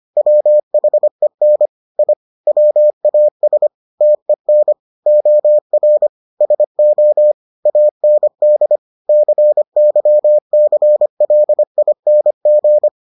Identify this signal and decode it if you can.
Morse code, heard as a keyed tone, about 25 wpm, 600 Hz. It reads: WHEN I WAS TEN OR SO AND CYCLING